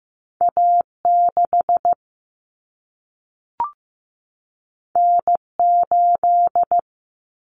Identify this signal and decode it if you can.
Morse code, heard as a keyed tone, about 15 wpm, 700 Hz. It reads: A6 N8
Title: Common Cut Numbers - 2-Digit 15wpm